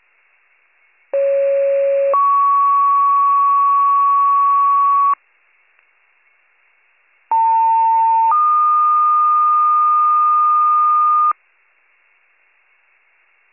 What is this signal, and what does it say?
Two-tone page, heard as a one-second tone, then a three-second tone, about 550 then 1120 Hz, about 900 then 1220 Hz. It is heard from 1.1 to 11.3 s.